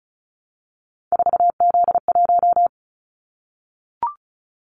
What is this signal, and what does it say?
Morse code, heard as a keyed tone, about 36 wpm, 690 Hz. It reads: 4Z1
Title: Callsign Prefixes for Top 50 DXCC - 35wpm